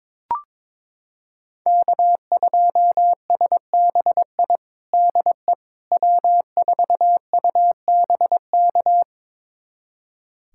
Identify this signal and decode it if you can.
Morse code, heard as a keyed tone, about 22 wpm, 700 Hz. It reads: K2SBI DE W4UBK
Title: USA Callsign DE Callsign - 5 Characters - 22wpm